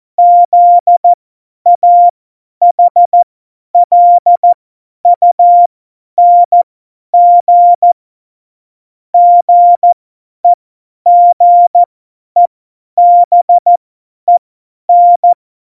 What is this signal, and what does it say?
Morse code, heard as a keyed tone, about 14 wpm, spaced as for 10 wpm, 700 Hz. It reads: ZAHLUNG GEGEBEN
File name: Morsekurs / Witze / Witz 07 Juli 2020, 50 F70